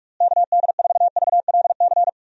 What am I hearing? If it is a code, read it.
Morse code, heard as a keyed tone, about 46 wpm, 700 Hz. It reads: KD4VLC
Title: USA Call Signs - 6 Characters - Repeated 2x - 45wpm